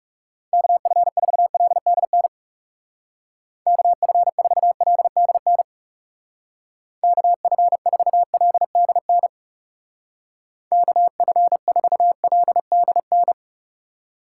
Morse code, heard as a keyed tone, about 37 wpm, 700 Hz. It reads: KF4LDN KF4LDN KF4LDN KF4LDN
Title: Speed Racing - USA 2x3 Call Signs 30wpm